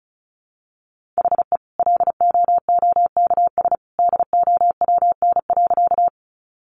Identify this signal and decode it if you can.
Morse code, heard as a keyed tone, about 34 wpm, 700 Hz. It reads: HE LOOKS DOWN.
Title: Additional Easy Phrases 35wpm